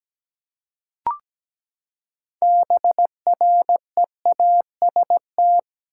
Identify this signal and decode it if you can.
Morse code, heard as a keyed tone, about 17 wpm, 700 Hz. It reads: BREAST
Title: Six Letter Words - 17wpm